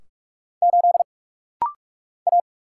Morse code, heard as a keyed tone, about 44 wpm, 700 Hz. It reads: 8 A
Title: Single Letter-Number Alt - Rapid-Fire 45wpm